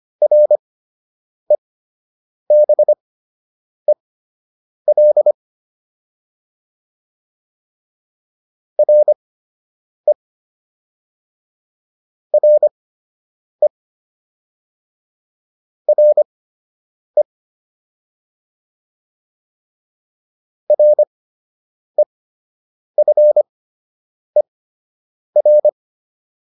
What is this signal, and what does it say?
Morse code, heard as a keyed tone, about 25 wpm, spaced as for 8 wpm, 600 Hz. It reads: REBEL RE RE RE REFER